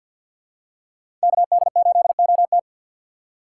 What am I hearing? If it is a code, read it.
Morse code, heard as a keyed tone, about 52 wpm, 700 Hz. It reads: KD8OT
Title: USA Call Signs - 5 Characters - 6x Character Spacing - 50wpm